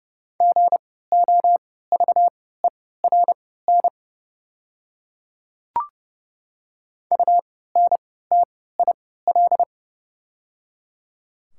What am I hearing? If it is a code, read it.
Morse code, heard as a keyed tone, about 30 wpm, spaced as for 17 wpm, 690 Hz. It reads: GOVERN UNTIL
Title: 3x Character Spacing - Top 500 Words - 30wpm